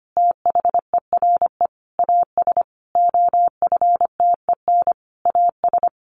Morse code, heard as a keyed tone, about 25 wpm, 700 Hz. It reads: THERE AS OFTEN AS
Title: Sentences from Top 1400 Words 25wpm